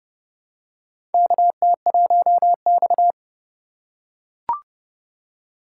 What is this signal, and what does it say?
Morse code, heard as a keyed tone, about 32 wpm, 700 Hz. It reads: KT1X
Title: USA Call Signs - 4 Characters - 3x Character Spacing - 30wpm